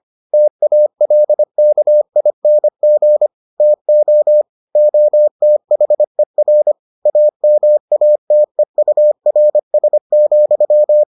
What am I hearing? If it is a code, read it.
Morse code, heard as a keyed tone, about 25 wpm, 600 Hz. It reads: TALKING TO OTHER AMATEURS,